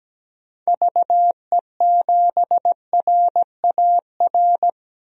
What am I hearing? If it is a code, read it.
Morse code, heard as a keyed tone, about 17 wpm, 700 Hz. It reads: VE7RAR
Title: DXCC-100 Call Signs - 6 Characters - 17wpm